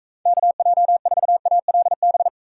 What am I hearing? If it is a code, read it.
Morse code, heard as a keyed tone, about 42 wpm, 700 Hz. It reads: KJ4ALB